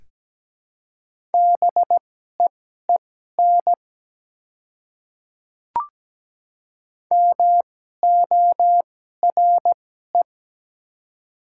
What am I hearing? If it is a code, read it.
Morse code, heard as a keyed tone, about 17 wpm, spaced as for 12 wpm, 700 Hz. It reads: BEEN MORE
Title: 2x Character Spacing - Top 300 Words - 17wpm